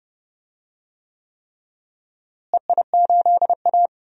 Morse code, heard as a keyed tone, about 30 wpm, 700 Hz. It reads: EI8A
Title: DXCC-100 Call Signs - 4 Characters - Repeated 3x - 30wpm